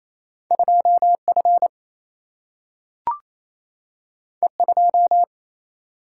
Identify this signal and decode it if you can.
Morse code, heard as a keyed tone, about 28 wpm, 700 Hz. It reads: 2F E2
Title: Single-Letter Single-Number 28wpm